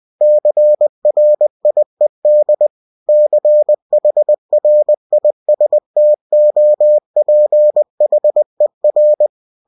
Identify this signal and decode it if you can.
Morse code, heard as a keyed tone, about 20 wpm, 600 Hz. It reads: CRIED CHRISTOPHER